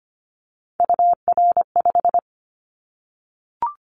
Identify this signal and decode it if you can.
Morse code, heard as a keyed tone, about 25 wpm, 700 Hz. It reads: UR5
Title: Callsign Prefixes for Top 25 DXCC - 25wpm